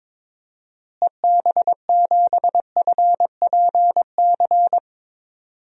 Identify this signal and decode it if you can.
Morse code, heard as a keyed tone, about 22 wpm, 700 Hz. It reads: EB7FPC
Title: DXCC-100 Call Signs - 6 Characters - Repeated 3x - 22wpm